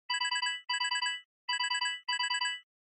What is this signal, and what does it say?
There is no code.